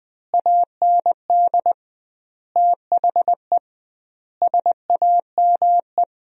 Morse code, heard as a keyed tone, about 20 wpm, 700 Hz. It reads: AND THE SAME
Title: 2x Word-Spacing - Sentences from Top 500 Words - 20wpm